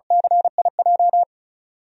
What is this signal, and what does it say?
Morse code, heard as a keyed tone, about 34 wpm, 700 Hz. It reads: CIJ
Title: DXCC-100 Call Signs - 5 Characters - 35wpm